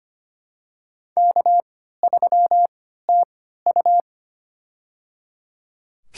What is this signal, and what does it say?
Morse code, heard as a keyed tone, about 25 wpm, spaced as for 14 wpm, 700 Hz. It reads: K3TU
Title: USA Call Signs - 3x Character Spacing - 25wpm